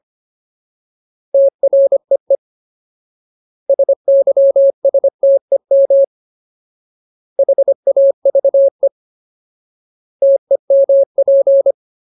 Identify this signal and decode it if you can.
Morse code, heard as a keyed tone, about 25 wpm, 550 Hz. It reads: TREE SYSTEM HAVE TEMP
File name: Ultimate Word List / 25WPM Ultimate